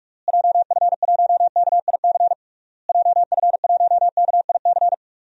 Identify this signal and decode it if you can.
Morse code, heard as a keyed tone, about 41 wpm, 700 Hz. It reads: JF1KIC JF1KIC
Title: DXCC-100 Call Signs - 6 Characters - Repeated 3x - 45wpm